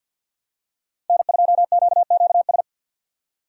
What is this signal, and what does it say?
Morse code, heard as a keyed tone, about 46 wpm, 700 Hz. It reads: N2QQS
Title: USA Call Signs - 5 Characters - 3x Character Spacing - 50wpm